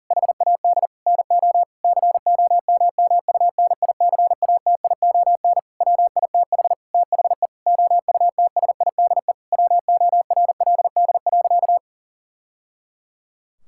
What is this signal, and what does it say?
Morse code, heard as a keyed tone, about 40 wpm, 700 Hz. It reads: HAD NO COMMUNICATION WITH THE OUTSIDE WORLD.
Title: Sentences from Top 1000 Words - Review Part 6 - 40wpm